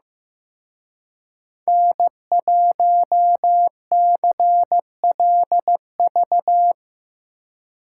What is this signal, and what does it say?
Morse code, heard as a keyed tone, about 17 wpm, 700 Hz. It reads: N1CLV
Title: Speed Racing - USA 1x3 Call Signs 15wpm